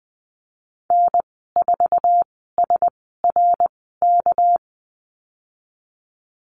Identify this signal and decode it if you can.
Morse code, heard as a keyed tone, about 19 wpm, spaced as for 14 wpm, 700 Hz. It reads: N4SRK
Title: 219 Lesson 35g - Number 0 - Call Signs 20wpm